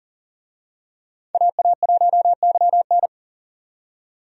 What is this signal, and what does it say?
Morse code, heard as a keyed tone, about 40 wpm, 700 Hz. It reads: AA1YN
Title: USA Call Signs - 5 Characters - 40wpm